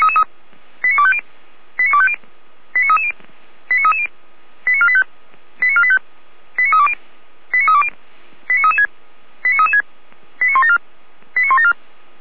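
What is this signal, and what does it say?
Five-tone selective call (ZVEI), heard as quick runs of five tones, tones of about 70 ms. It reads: 89250 89250 89330 89330 89485 89485 89320 89320 89336 89336 89174 89174
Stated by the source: ZVEI five-tone selective call 89303 received on SDR.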